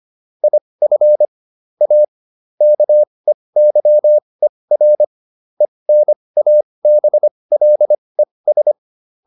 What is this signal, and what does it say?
Morse code, heard as a keyed tone, about 25 wpm, spaced as for 20 wpm, 600 Hz. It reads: IF A KEYER ENABLES
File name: SS316 20